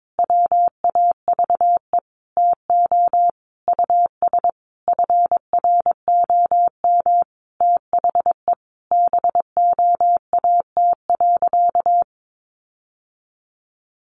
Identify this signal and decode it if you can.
Morse code, heard as a keyed tone, about 22 wpm, 700 Hz. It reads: WAVE TO US FROM THE BOAT.
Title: Sentences from Top 1400 Words 22wpm